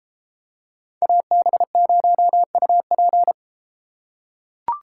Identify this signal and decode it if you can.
Morse code, heard as a keyed tone, about 30 wpm, 700 Hz. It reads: AB0UP